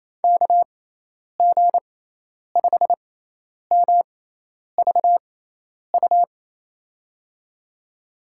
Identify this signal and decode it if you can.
Morse code, heard as a keyed tone, about 28 wpm, spaced as for 10 wpm, 700 Hz. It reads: KG5MVU
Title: USA Call Signs - 6 Characters - 6x Character Spacing - 28wpm